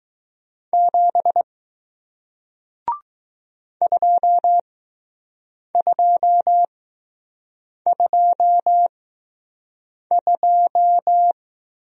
Morse code, heard as a keyed tone, about 20 wpm, spaced as for 6 wpm, 700 Hz. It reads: 7 2222